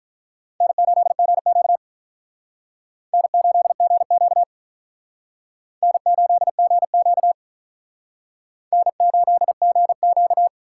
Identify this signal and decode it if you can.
Morse code, heard as a keyed tone, about 43 wpm, 700 Hz. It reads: N8GQ N8GQ N8GQ N8GQ
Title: Speed Racing - USA 1x2 Call Signs 35wpm